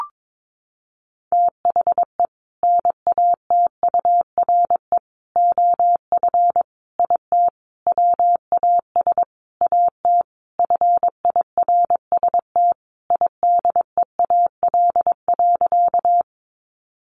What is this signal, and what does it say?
Morse code, heard as a keyed tone, about 22 wpm, 700 Hz. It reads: THE NATURE OF IT WAS AT FIRST IDEAL.
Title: Sentences from Top 1400 Words 22wpm